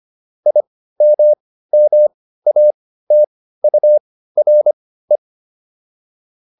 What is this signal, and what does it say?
Morse code, heard as a keyed tone, about 25 wpm, spaced as for 15 wpm, 600 Hz. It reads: IMMATURE